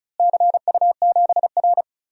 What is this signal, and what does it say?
Morse code, heard as a keyed tone, about 36 wpm, 700 Hz. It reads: CU7R
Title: DXCC-100 Call Signs - 4 Characters - Repeated 2x - 35wpm